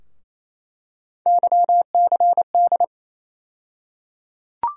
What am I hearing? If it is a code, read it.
Morse code, heard as a keyed tone, about 28 wpm, 700 Hz. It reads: YCD
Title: USA Call Sign Postfixes - 3 Letters 28wpm